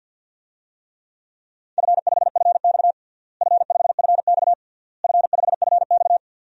Morse code, heard as a keyed tone, about 52 wpm, 700 Hz. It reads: F5FX F5FX F5FX
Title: DXCC-100 Call Signs - 4 Characters - Repeated 3x - 50wpm